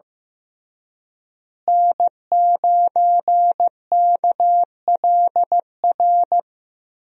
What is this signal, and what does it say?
Morse code, heard as a keyed tone, about 15 wpm, 700 Hz. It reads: N9KLR
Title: USA Call Signs - 5 Characters - Repeated 3x - 15wpm